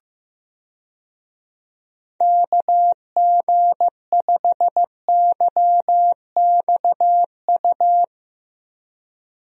Morse code, heard as a keyed tone, about 16 wpm, 700 Hz. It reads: KG5YXU
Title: USA Call Signs - 6 Characters - Repeated 3x - 15wpm